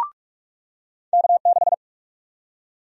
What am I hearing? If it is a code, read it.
Morse code, heard as a keyed tone, about 42 wpm, 700 Hz. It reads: K6
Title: USA Call Sign Prefixes - 45wpm